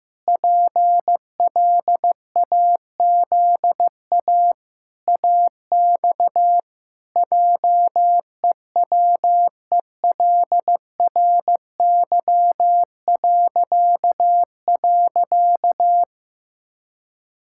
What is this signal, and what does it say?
Morse code, heard as a keyed tone, about 15 wpm, 700 Hz. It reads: PLAZA AX JEWELRY..
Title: Pangram Challenge v2 - 15wpm